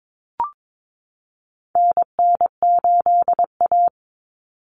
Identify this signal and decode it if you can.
Morse code, heard as a keyed tone, about 22 wpm, 700 Hz. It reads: NN8A
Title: Common Cut Numbers - 4-Digit 22wpm